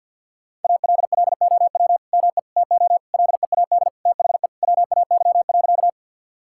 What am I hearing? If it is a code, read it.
Morse code, heard as a keyed tone, about 50 wpm, 700 Hz. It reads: ALLOW ME TO LEAD THE WAY.
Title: Sentences from Top 1000 Words - Review Part 5 - 50wpm